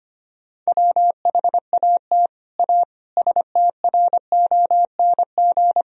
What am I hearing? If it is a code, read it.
Morse code, heard as a keyed tone, about 25 wpm, 700 Hz. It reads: WHAT A STRONG
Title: Sentences from Top 1000 Words - Review Part 3 - 25wpm